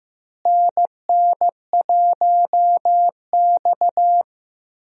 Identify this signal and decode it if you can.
Morse code, heard as a keyed tone, about 15 wpm, 700 Hz. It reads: NN1X
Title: USA Call Signs - 4 Characters - Repeated 2x - 15wpm